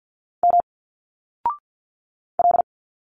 Morse code, heard as a keyed tone, about 50 wpm, 700 Hz. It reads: M L
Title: Single Letters 50wpm